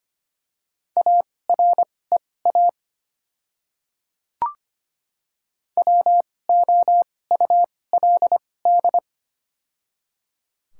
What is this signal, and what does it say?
Morse code, heard as a keyed tone, about 25 wpm, spaced as for 18 wpm, 700 Hz. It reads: AREA WOULD
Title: 2x Character Spacing - Top 500 Words - 25wpm